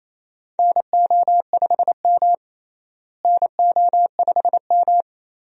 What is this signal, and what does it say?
Morse code, heard as a keyed tone, about 27 wpm, 700 Hz. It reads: NO5M NO5M
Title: USA Call Signs - 4 Characters - Repeated 3x - 28wpm